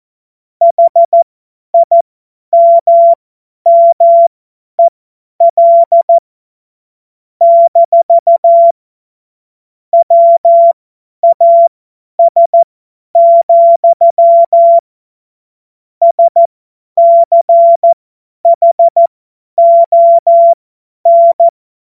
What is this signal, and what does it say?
Morse code, heard as a keyed tone, about 14 wpm, spaced as for 10 wpm, 680 Hz. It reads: HIMMEL - WAS, SCHON
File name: Morsekurs / Witze / Witz 18. Januar 2022, 50 F70